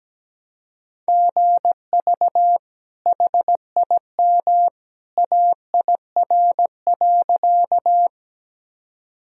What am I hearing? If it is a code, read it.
Morse code, heard as a keyed tone, about 17 wpm, 700 Hz. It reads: GV HIM AIR.
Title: Sentences from Top 200 Words and Abbreviations 17wpm